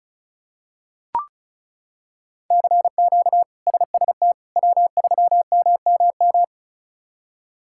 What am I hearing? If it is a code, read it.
Morse code, heard as a keyed tone, about 36 wpm, 700 Hz. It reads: CQ SST W3MMM
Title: K1USN SST QSO Practice - 35wpm